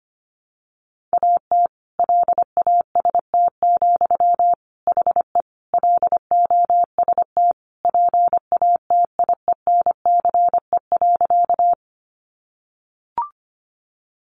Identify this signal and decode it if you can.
Morse code, heard as a keyed tone, about 25 wpm, 700 Hz. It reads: AT LAST, HE LOST PATIENCE.
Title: Sentences from Top 1400 Words 25wpm